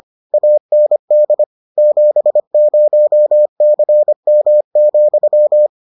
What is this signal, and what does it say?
Morse code, heard as a keyed tone, about 25 wpm, 600 Hz. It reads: AND 70CM,